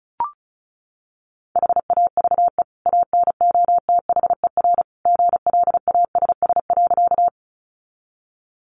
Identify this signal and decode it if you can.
Morse code, heard as a keyed tone, about 34 wpm, 700 Hz. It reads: HAVE ANOTHER GLASS.
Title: Sentences from Top 1300 Words 35wpm